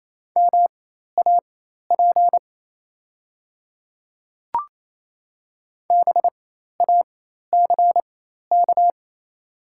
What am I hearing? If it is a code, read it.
Morse code, heard as a keyed tone, about 28 wpm, spaced as for 13 wpm, 700 Hz. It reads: MAP BACK
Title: 4x Character Spacing - Top 500 Words - 28wpm